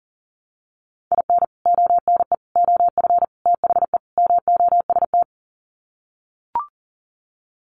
Morse code, heard as a keyed tone, about 40 wpm, 700 Hz. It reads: IN ONE OF THE MOST
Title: Sets of 5 Words Encore 40wpm